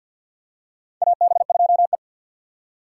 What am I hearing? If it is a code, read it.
Morse code, heard as a keyed tone, about 47 wpm, 700 Hz. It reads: AB2E
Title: USA Call Signs - 4 Characters - 6x Character Spacing - 50wpm